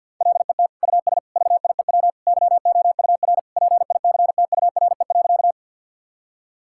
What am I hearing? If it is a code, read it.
Morse code, heard as a keyed tone, about 50 wpm, 700 Hz. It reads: LET US VIEW YOUR PICTURE.
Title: Sentences from Top 1000 Words - Review Part 6 - 50wpm